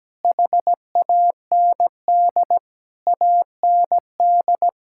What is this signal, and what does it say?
Morse code, heard as a keyed tone, about 17 wpm, 700 Hz. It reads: HAND AND
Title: Sentences from Top 1600 Words 17wpm